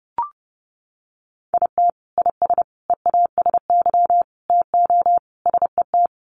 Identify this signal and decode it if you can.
Morse code, heard as a keyed tone, about 30 wpm, 700 Hz. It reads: IT IS EASY TO SET
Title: Sentences from Top 1300 Words 30wpm